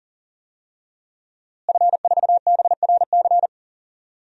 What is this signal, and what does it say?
Morse code, heard as a keyed tone, about 40 wpm, 700 Hz. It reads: F4BRC
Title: DXCC-100 Call Signs - 5 Characters - Repeated 2x - 40wpm